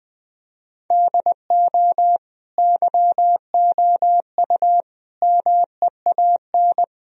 Morse code, heard as a keyed tone, about 21 wpm, 700 Hz. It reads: DO YOU MEAN
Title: Sentences from Top 1000 Words - Review Part 4 - 20wpm